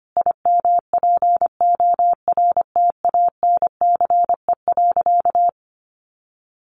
Morse code, heard as a keyed tone, about 25 wpm, 700 Hz. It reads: IMPORTANCE.
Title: Sentences from Top 1200 Words 25wpm